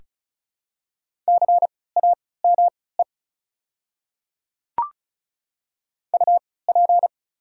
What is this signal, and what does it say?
Morse code, heard as a keyed tone, about 34 wpm, spaced as for 20 wpm, 700 Hz. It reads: CAME UP
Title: 3x Character Spacing - Top 500 Words - 35wpm